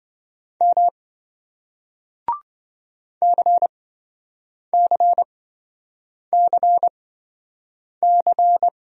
Morse code, heard as a keyed tone, about 27 wpm, spaced as for 7 wpm, 700 Hz. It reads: M CCCC